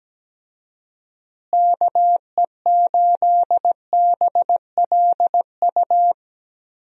Morse code, heard as a keyed tone, about 17 wpm, 700 Hz. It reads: KE8BLU
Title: USA Call Signs - 6 Characters - Repeated 3x - 17wpm